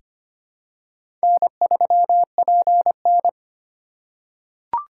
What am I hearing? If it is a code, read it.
Morse code, heard as a keyed tone, about 25 wpm, 700 Hz. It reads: N3PN
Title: USA Call Signs - 3x Character Spacing - 25wpm